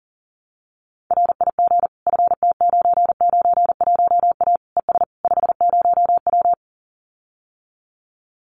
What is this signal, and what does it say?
Morse code, heard as a keyed tone, about 40 wpm, 700 Hz. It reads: RIG FT991A ES 50W
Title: Common QSO Phrases Encore 40wpm